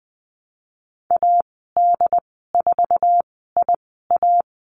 Morse code, heard as a keyed tone, about 20 wpm, spaced as for 15 wpm, 700 Hz. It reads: AD4IA
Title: 076 Lesson 13g - Letter L - Call Signs 20wpm